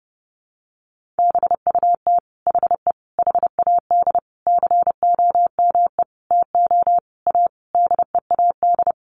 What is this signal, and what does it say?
Morse code, heard as a keyed tone, about 30 wpm, 700 Hz. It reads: BUT HE HAD COME TO A DEAD
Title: Sentences from Top 1500 Words 30wpm